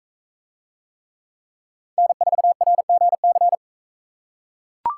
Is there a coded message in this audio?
Yes: Morse code from 2.0 to 3.6 s, heard as a keyed tone, about 44 wpm, 700 Hz. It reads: N4RGC